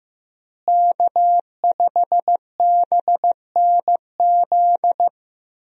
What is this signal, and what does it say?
Morse code, heard as a keyed tone, about 15 wpm, 700 Hz. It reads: K5BNZ
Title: USA Call Signs - 5 Characters - Repeated 2x - 15wpm